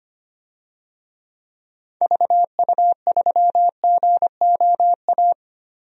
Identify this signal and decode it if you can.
Morse code, heard as a keyed tone, about 25 wpm, 700 Hz. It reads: VU3GOA
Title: DXCC-100 Call Signs - 6 Characters - Repeated 2x - 25wpm